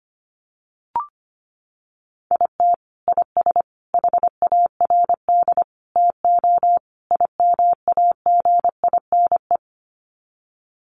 Morse code, heard as a keyed tone, about 25 wpm, 700 Hz. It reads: IT IS HARD TO IMAGINE
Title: Sets of 5 Words Encore2 25wpm